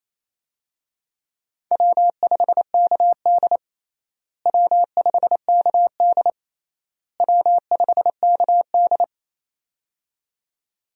Morse code, heard as a keyed tone, about 28 wpm, 700 Hz. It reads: W5KD W5KD W5KD
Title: USA Call Signs - 4 Characters - Repeated 3x - 28wpm